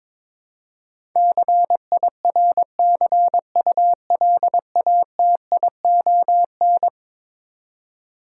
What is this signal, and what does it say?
Morse code, heard as a keyed tone, about 22 wpm, 700 Hz. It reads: CIRCULATION
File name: practice / 22wpm / Eleven Letter Words - 22wpm